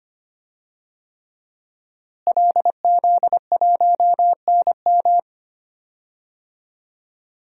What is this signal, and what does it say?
Morse code, heard as a keyed tone, about 25 wpm, 700 Hz. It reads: LZ1NM
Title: DXCC-100 Call Signs - 5 Characters - Repeated 3x - 25wpm